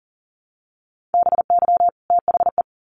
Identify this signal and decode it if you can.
Morse code, heard as a keyed tone, about 40 wpm, 700 Hz. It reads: BY THE
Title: Sets of 2 Words 40wpm